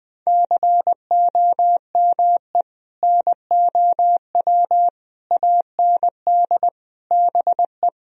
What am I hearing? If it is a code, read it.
Morse code, heard as a keyed tone, about 20 wpm, 700 Hz. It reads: COME NOW AND BE
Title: Sentences from Top 1000 Words - Review Part 4 - 20wpm